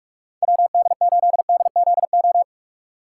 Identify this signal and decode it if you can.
Morse code, heard as a keyed tone, about 44 wpm, 700 Hz. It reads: WD8DZO
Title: USA Call Signs - 6 Characters - 4x Character Spacing - 45wpm